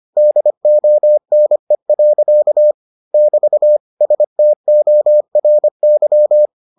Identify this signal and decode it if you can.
Morse code, heard as a keyed tone, about 25 wpm, 600 Hz. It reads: DONE. <BT> STORY